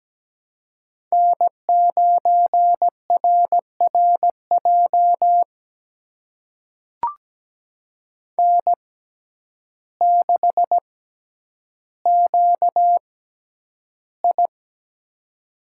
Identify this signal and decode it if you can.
Morse code, heard as a keyed tone, about 17 wpm, spaced as for 9 wpm, 700 Hz. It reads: N9RRJ N6QI
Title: USA Call Signs - 6x Character Spacing - 17wpm